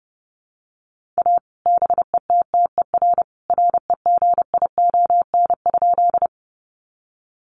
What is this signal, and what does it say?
Morse code, heard as a keyed tone, about 30 wpm, 700 Hz. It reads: A BETTER REGION?
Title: Sentences from Top 1500 Words 30wpm